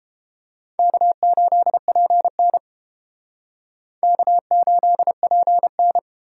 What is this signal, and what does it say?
Morse code, heard as a keyed tone, about 30 wpm, 700 Hz. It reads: K8PN K8PN